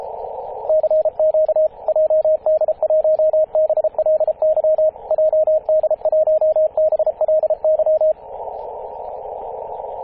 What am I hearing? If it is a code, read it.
Morse code, heard as a keyed tone, about 32 wpm, 620 Hz, with radio noise. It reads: CQ JD1BLY JD1BLY
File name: jd1bly-14-027mhz-cw